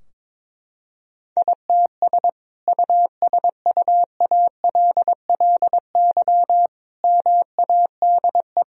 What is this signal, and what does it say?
Morse code, heard as a keyed tone, about 22 wpm, 700 Hz. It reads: ITS USUALLY MADE
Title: Sentences from Top 1000 Words - Review Part 3 - 22wpm